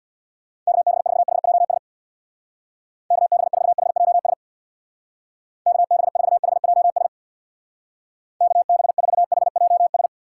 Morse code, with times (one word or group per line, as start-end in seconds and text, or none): 0.7-1.8 KB4HJS
3.1-4.3 KB4HJS
5.7-7.1 KB4HJS
8.4-10.1 KB4HJS